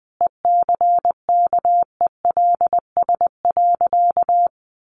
Morse code, heard as a keyed tone, about 20 wpm, 700 Hz. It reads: ECKELS.
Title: A Sound Of Thunder 20wpm